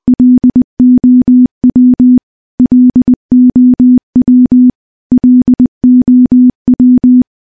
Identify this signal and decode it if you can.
Morse code, heard as a keyed tone, about 20 wpm, 250 Hz. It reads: LOW LOW LOW
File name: morse / morse-training-20WPM